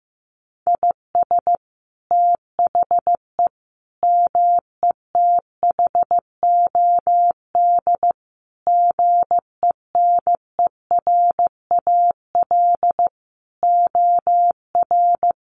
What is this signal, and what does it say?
Morse code, heard as a keyed tone, about 15 wpm, 700 Hz. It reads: IS THE METHOD GENERAL OR
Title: Sentences from Top 1500 Words 15wpm